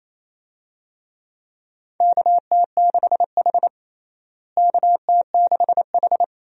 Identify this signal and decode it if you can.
Morse code, heard as a keyed tone, about 28 wpm, 700 Hz. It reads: KT6H KT6H
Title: USA Call Signs - 4 Characters - Repeated 3x - 28wpm